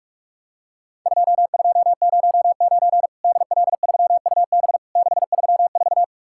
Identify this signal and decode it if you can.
Morse code, heard as a keyed tone, about 47 wpm, 700 Hz. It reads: 2209 DL3UB 634
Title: Worked All Europe DX Contest - QTC Exchange - 45wpm